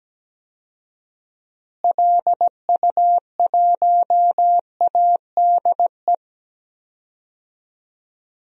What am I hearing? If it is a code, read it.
Morse code, heard as a keyed tone, about 17 wpm, 700 Hz. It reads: LU1ADE
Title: DXCC-100 Call Signs - 6 Characters - Repeated 2x - 17wpm